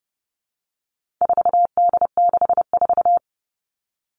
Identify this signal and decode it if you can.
Morse code, heard as a keyed tone, about 30 wpm, 700 Hz. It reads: 4D64
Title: All Cut Numbers - 4-Digit 30wpm